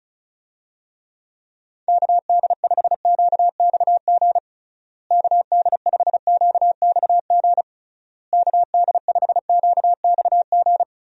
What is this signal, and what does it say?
Morse code, heard as a keyed tone, about 34 wpm, 700 Hz. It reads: KD5QXG KD5QXG KD5QXG
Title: USA Call Signs - 6 Characters - Repeated 3x - 35wpm